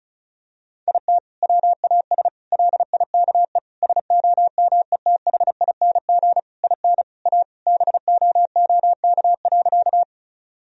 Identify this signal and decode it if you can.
Morse code, heard as a keyed tone, about 34 wpm, 700 Hz. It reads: IT WAS LIKE SOMETHING IN A BOOK.
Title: Sentences from Top 1000 Words - Review Part 1 - 35wpm